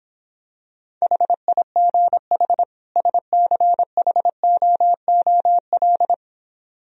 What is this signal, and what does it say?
Morse code, heard as a keyed tone, about 26 wpm, 700 Hz. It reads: HIGH SCHOOL